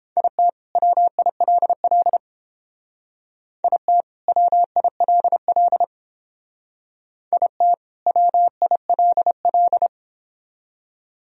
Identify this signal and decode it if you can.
Morse code, heard as a keyed tone, about 31 wpm, 700 Hz. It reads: IT WILL IT WILL IT WILL